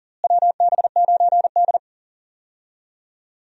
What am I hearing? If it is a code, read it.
Morse code, heard as a keyed tone, about 40 wpm, 700 Hz. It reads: WB9D
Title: USA Call Signs - 4 Characters - Repeated 2x - 40wpm